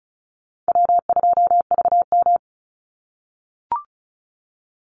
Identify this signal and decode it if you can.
Morse code, heard as a keyed tone, about 34 wpm, 700 Hz. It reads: W2VM
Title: Call Signs - 35wpm